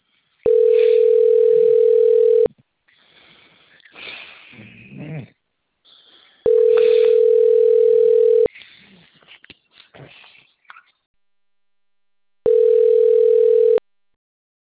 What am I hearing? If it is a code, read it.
Ringback tone (North America), 440 + 480 Hz.
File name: rg-1600-unknown- - - .93979